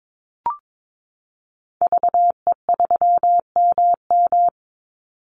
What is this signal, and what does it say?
Morse code, heard as a keyed tone, about 22 wpm, 700 Hz. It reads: VE3MM
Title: Call Signs - 22wpm